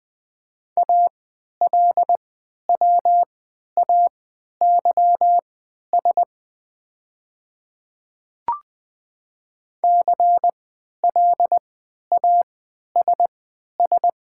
Morse code, heard as a keyed tone, about 20 wpm, spaced as for 11 wpm, 700 Hz. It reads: ALWAYS CLASS
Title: 3x Character Spacing - Top 500 Words - 20wpm